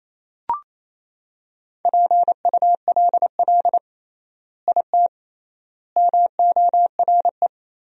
Morse code, heard as a keyed tone, about 28 wpm, 700 Hz. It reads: PULL IT MORE
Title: 3x Word-Spacing - Sentences from Top 400 Words - 28wpm